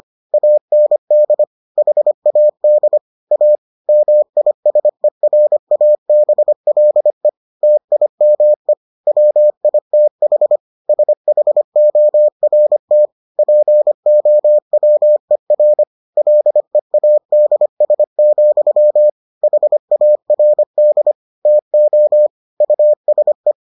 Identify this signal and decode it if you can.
Morse code, heard as a keyed tone, about 25 wpm, 600 Hz. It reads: AND HAD A MISERABLE TIME WITH SHORT POWER LEADS, HARD TO USE